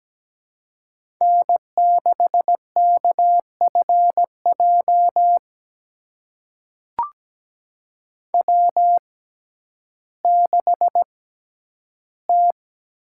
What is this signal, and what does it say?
Morse code, heard as a keyed tone, about 17 wpm, spaced as for 10 wpm, 700 Hz. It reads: N6KFJ W6T
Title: USA Call Signs - 5 Characters - 6x Character Spacing - 17wpm